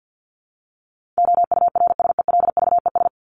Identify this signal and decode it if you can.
Morse code, heard as a keyed tone, about 50 wpm, 700 Hz. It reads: OURSELVES
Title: Top 1501-1750 Words 50wpm